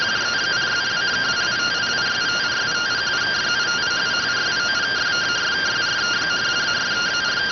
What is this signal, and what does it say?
Two-tone teleprinter signal, about 170 Hz shift.